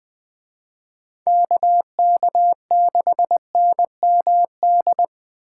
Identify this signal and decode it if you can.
Morse code, heard as a keyed tone, about 21 wpm, 700 Hz. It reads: KK6NMD
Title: Speed Racing - USA Call Signs 17wpm